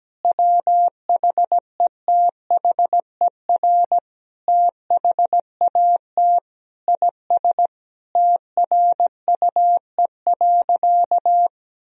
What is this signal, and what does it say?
Morse code, heard as a keyed tone, about 17 wpm, 700 Hz. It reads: WHETHER THAT IS TRUE.
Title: Sentences from Top 1000 Words - Review Part 3 - 17wpm